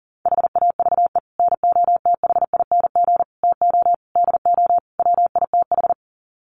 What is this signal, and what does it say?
Morse code, heard as a keyed tone, about 40 wpm, 700 Hz. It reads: HAVE NOTHING TO DO WITH
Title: Sets of 5 Words 40wpm